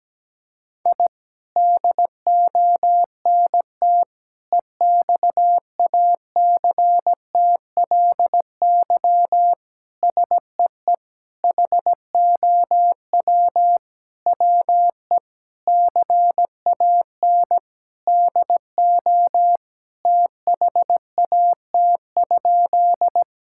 Morse code, heard as a keyed tone, about 17 wpm, 700 Hz. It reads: I DONT EXACTLY SEE HOW WE CAN DO THAT?
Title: Sentences from Top 1000 Words - Review Part 6 - 17wpm